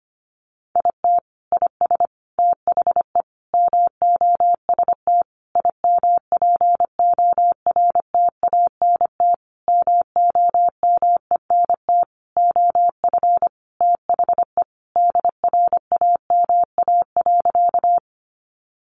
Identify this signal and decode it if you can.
Morse code, heard as a keyed tone, about 25 wpm, 700 Hz. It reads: IT IS THE MOST IMPORTANT MOMENT OF THE DRAMA.
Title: Sentences from Top 1500 Words 25wpm